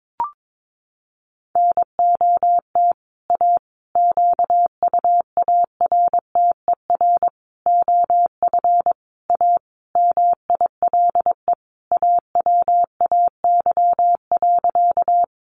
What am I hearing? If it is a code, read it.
Morse code, heard as a keyed tone, about 22 wpm, 700 Hz. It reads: NOT A QUARTER OF A MILE AWAY.
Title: Sentences from Top 1300 Words 22wpm